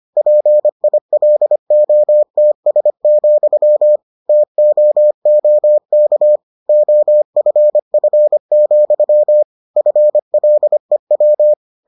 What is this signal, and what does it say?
Morse code, heard as a keyed tone, about 25 wpm, 600 Hz. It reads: PILOTS, TOOK OFF, FLEW